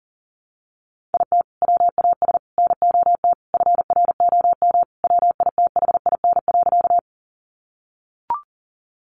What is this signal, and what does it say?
Morse code, heard as a keyed tone, about 40 wpm, 700 Hz. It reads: IT WAS NOT FROM WITHIN.
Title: Sentences from Top 700 Words 40wpm